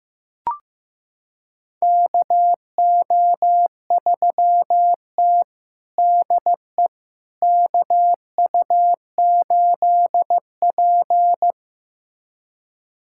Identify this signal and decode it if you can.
Morse code, heard as a keyed tone, about 15 wpm, 700 Hz. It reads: KO3T DE KU8P
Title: USA Callsign DE Callsign - 4 Characters - 15wpm